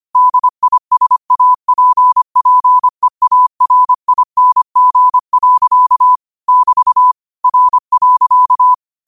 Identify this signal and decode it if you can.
Morse code, heard as a keyed tone, about 25 wpm, 1000 Hz. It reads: DISAPPEARING. <BT> R.
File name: qotdfiles / 25wpm